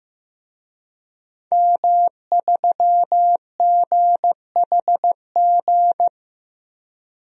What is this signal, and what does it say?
Morse code, heard as a keyed tone, about 15 wpm, 700 Hz. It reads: M3GHG
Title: DXCC-100 Call Signs - 5 Characters - Repeated 2x - 15wpm